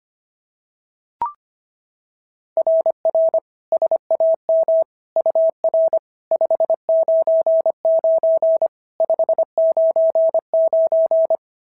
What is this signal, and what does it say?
Morse code, heard as a keyed tone, about 24 wpm, 650 Hz. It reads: RR SAM UR 599 599
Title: SOTA QSO Practice 25wpm